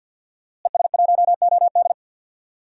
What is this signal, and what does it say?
Morse code, heard as a keyed tone, about 51 wpm, 700 Hz. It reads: ES1OD
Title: DXCC-100 Call Signs - 5 Characters - Repeated 2x - 50wpm